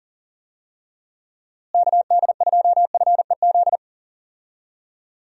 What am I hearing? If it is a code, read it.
Morse code, heard as a keyed tone, about 38 wpm, 700 Hz. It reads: KD2FEZ
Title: USA Call Signs - 6 Characters - Repeated 2x - 40wpm